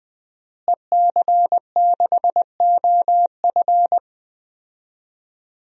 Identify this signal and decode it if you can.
Morse code, heard as a keyed tone, about 20 wpm, 700 Hz. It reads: EC6OF
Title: DXCC-100 Call Signs - 5 Characters - Repeated 2x - 20wpm